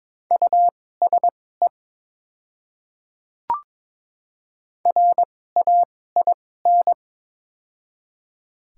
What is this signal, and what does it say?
Morse code, heard as a keyed tone, about 21 wpm, spaced as for 16 wpm, 700 Hz. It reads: USE RAIN
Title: 2x Character Spacing - Top 300 Words - 22wpm